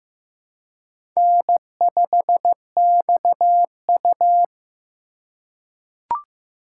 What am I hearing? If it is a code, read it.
Morse code, heard as a keyed tone, about 15 wpm, 700 Hz. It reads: N5XU
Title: USA Call Signs - 4 Characters - 3x Character Spacing - 15wpm